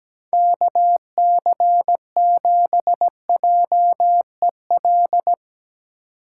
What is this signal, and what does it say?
Morse code, heard as a keyed tone, about 17 wpm, 700 Hz. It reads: KC7JEL
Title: USA Call Signs - 6 Characters - 3x Character Spacing - 17wpm